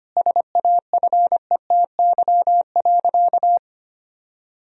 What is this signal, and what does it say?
Morse code, heard as a keyed tone, about 25 wpm, 700 Hz. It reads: SAFETY.
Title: Sentences from Top 1000 Words - Review Part 4 - 25wpm